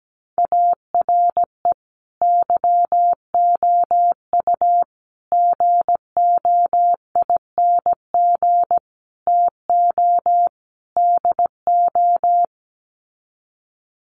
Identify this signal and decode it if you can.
Morse code, heard as a keyed tone, about 17 wpm, 700 Hz. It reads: ARE YOU GOING TO DO
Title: Sets of 5 Words 17wpm